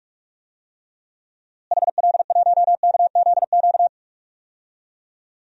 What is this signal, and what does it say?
Morse code, heard as a keyed tone, about 46 wpm, 700 Hz. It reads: HL1KZQ
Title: DXCC-100 Call Signs - 6 Characters - Repeated 2x - 45wpm